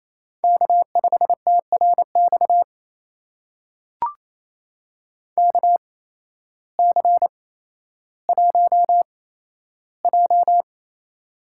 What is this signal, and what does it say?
Morse code, heard as a keyed tone, about 28 wpm, spaced as for 13 wpm, 700 Hz. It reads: K5TRX KC1J
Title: USA Call Signs - 8x Character Spacing - 28wpm